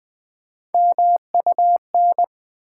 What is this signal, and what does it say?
Morse code, heard as a keyed tone, about 20 wpm, 700 Hz. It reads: MUN
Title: Build A Word - 3 Letters At A Time - 9-Letter Words - 20wpm